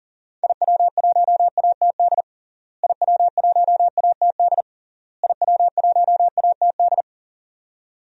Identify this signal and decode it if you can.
Morse code, heard as a keyed tone, about 40 wpm, 700 Hz. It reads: IW1ATD IW1ATD IW1ATD
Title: DXCC-100 Call Signs - 6 Characters - Repeated 3x - 40wpm